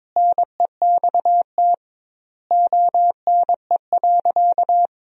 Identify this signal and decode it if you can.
Morse code, heard as a keyed tone, about 22 wpm, 700 Hz. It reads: NEXT ONE.
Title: 2x Word-Spacing - Sentences from Top 400 Words - 22wpm